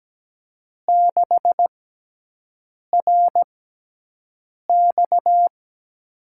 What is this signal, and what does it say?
Morse code, heard as a keyed tone, about 17 wpm, spaced as for 6 wpm, 700 Hz. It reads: 6RX
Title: USA Call Signs - 4 Characters - 6x Character Spacing - 17wpm